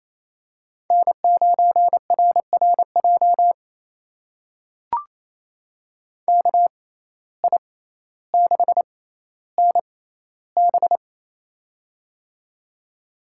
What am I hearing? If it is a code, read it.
Morse code, heard as a keyed tone, about 28 wpm, spaced as for 14 wpm, 700 Hz. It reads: N9RRJ KI6NB
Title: USA Call Signs - 5 Characters - 6x Character Spacing - 28wpm